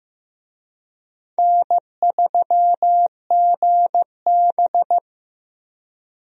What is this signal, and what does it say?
Morse code, heard as a keyed tone, about 15 wpm, 700 Hz. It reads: N3GB
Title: USA Call Signs - 4 Characters - 15wpm